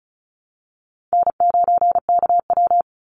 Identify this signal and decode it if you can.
Morse code, heard as a keyed tone, about 34 wpm, 700 Hz. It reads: N9KW
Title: NA SOTA Chaser Callsigns 35wpm